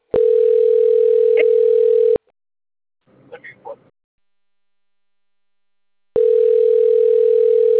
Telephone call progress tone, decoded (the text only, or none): ringback tone